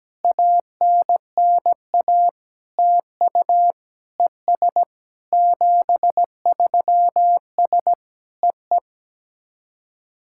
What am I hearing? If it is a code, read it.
Morse code, heard as a keyed tone, about 17 wpm, 700 Hz. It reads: ANNA TU ES 73S EE
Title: SOTA QSO Practice 17wpm